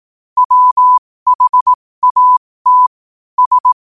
Morse code, heard as a keyed tone, about 18 wpm, spaced as for 15 wpm, 1000 Hz. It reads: WHAT S
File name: qotdfiles / 15wpm